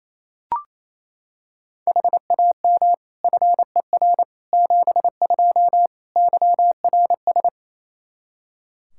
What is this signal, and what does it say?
Morse code, heard as a keyed tone, about 28 wpm, 700 Hz. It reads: HAM FER 72 YRS
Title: K1USN SST QSO Practice 28wpm